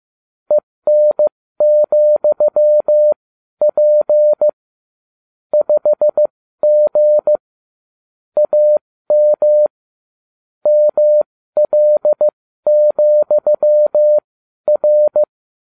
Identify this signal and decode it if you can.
Morse code, heard as a keyed tone, about 15 wpm, spaced as for 12 wpm, 600 Hz. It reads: EN,P 5G AM ML,R